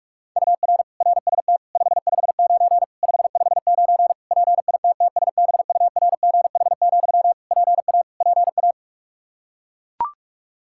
Morse code, heard as a keyed tone, about 41 wpm, 700 Hz. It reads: UR RST 559 559 PITTSBURGH, PA PA
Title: Practice QSO - Exchange 1 - Beginning - 45wpm